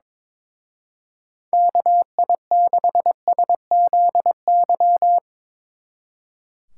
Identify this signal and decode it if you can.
Morse code, heard as a keyed tone, about 22 wpm, 700 Hz. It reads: KI6SZY
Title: Speed Racing - USA 2x3 Call Signs 22wpm